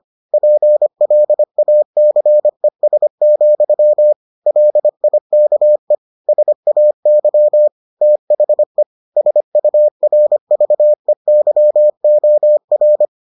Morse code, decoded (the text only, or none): PLACES, LIKE SAY THE SURVEYOR